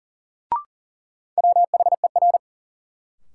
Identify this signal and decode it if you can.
Morse code, heard as a keyed tone, about 40 wpm, 700 Hz. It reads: WHER
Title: Sentences from Top 1500 Words - New Word Drill - 40wpm